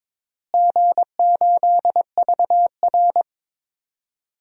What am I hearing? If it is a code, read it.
Morse code, heard as a keyed tone, about 22 wpm, 700 Hz. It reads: G8VR
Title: DXCC-100 Call Signs - 4 Characters - Repeated 3x - 22wpm